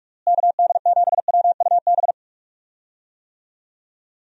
Morse code, heard as a keyed tone, about 41 wpm, 700 Hz. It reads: KD7WUB
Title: USA Call Signs - 6 Characters - Repeated 3x - 45wpm